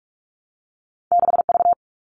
Morse code, heard as a keyed tone, about 46 wpm, 700 Hz. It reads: 6V
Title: All Cut Numbers - 2-Digit 45wpm